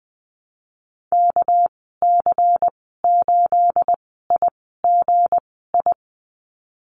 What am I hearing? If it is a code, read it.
Morse code, heard as a keyed tone, about 20 wpm, spaced as for 15 wpm, 700 Hz. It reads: KC8IGI
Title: 219 Lesson 35g - Number 0 - Call Signs 20wpm